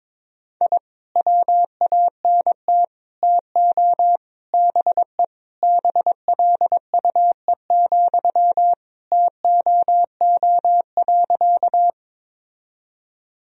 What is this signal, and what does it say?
Morse code, heard as a keyed tone, about 22 wpm, 700 Hz. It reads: I WANT TO BE BLUE, TOO.
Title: Sentences from Top 1000 Words - Review Part 3 - 22wpm